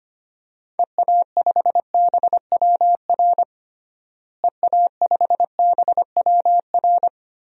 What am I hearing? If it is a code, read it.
Morse code, heard as a keyed tone, about 25 wpm, 700 Hz. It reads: EA5BWR EA5BWR
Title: DXCC-100 Call Signs - 6 Characters - Repeated 3x - 25wpm